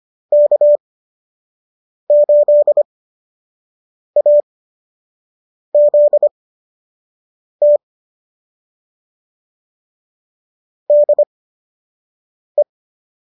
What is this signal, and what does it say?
Morse code, heard as a keyed tone, about 25 wpm, spaced as for 6 wpm, 590 Hz. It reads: K8AZT DE